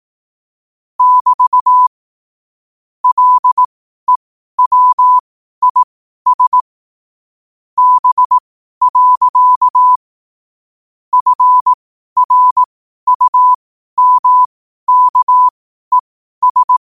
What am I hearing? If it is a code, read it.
Morse code, heard as a keyed tone, about 18 wpm, spaced as for 12 wpm, 1000 Hz. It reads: <BT> LEWIS B. FRUMKES
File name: qotdfiles / 12wpm